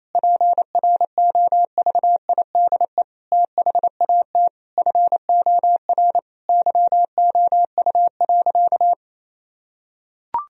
Morse code, heard as a keyed tone, about 28 wpm, 700 Hz. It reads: PROVIDE THAT FOR YOU.
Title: Sentences from Top 1000 Words - Review Part 3 - 28wpm